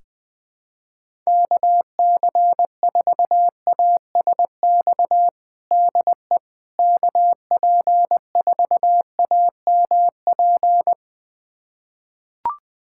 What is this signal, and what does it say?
Morse code, heard as a keyed tone, about 21 wpm, 700 Hz. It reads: KC4ASX DE KP4AMP
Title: USA Callsign DE Callsign 20wpm